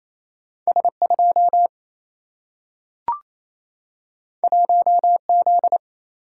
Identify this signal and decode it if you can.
Morse code, heard as a keyed tone, about 28 wpm, 700 Hz. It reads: S2 1Z
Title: Single-Letter Single-Number 28wpm